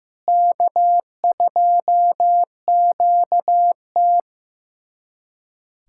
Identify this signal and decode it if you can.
Morse code, heard as a keyed tone, about 15 wpm, 700 Hz. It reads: K2QT
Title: USA Callsign DE Callsign - 4 Characters - 15wpm